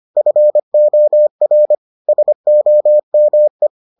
Morse code, heard as a keyed tone, about 25 wpm, 600 Hz. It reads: FOR SOME